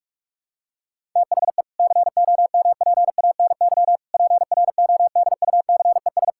Morse code, heard as a keyed tone, about 46 wpm, 700 Hz. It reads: THE COMPANY PRODUCES
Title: Sentences from Top 1000 Words - Review Part 6 - 45wpm